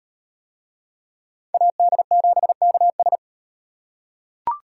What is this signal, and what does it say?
Morse code, heard as a keyed tone, about 36 wpm, 700 Hz. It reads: AD7KS